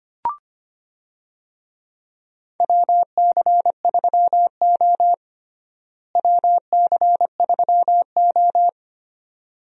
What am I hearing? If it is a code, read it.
Morse code, heard as a keyed tone, about 25 wpm, 700 Hz. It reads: WC3O WC3O
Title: USA Call Signs - 4 Characters - Repeated 2x - 25wpm